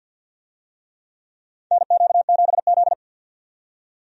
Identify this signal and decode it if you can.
Morse code, heard as a keyed tone, about 53 wpm, 700 Hz. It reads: NQ7Z
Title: USA Call Signs - 4 Characters - 50wpm